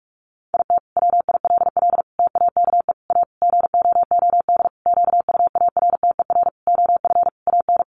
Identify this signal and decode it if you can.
Morse code, heard as a keyed tone, about 44 wpm, 700 Hz. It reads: IT WILL TAKE A GOOD QUARTER OF AN
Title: Sentences from Top 1300 Words 45wpm